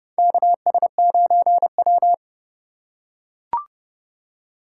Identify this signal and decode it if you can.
Morse code, heard as a keyed tone, about 32 wpm, 700 Hz. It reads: KS9W
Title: K1USN SST QSO Practice - 30wpm